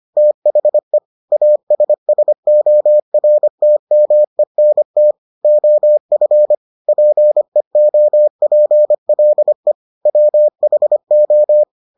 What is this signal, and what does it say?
Morse code, heard as a keyed tone, about 25 wpm, 600 Hz. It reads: THE ASSORTMENT OF PEOPLE WHO